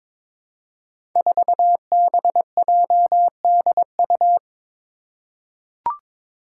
Morse code, heard as a keyed tone, about 22 wpm, 700 Hz. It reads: 4BJDU
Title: Five Challenging Letter-Number Combinations 22wpm